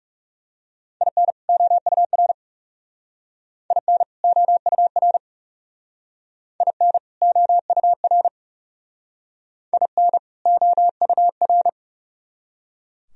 Morse code, heard as a keyed tone, about 36 wpm, 700 Hz. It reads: IN OUR IN OUR IN OUR IN OUR
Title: Speed-Racing - Sets of 2 Words 30wpm